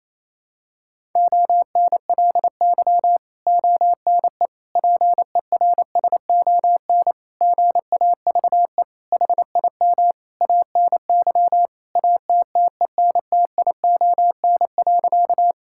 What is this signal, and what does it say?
Morse code, heard as a keyed tone, about 28 wpm, 700 Hz. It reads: ONLY ONE PERSON GAVE HIM ANY ATTENTION.
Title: Sentences from Top 1000 Words - Review Part 6 - 28wpm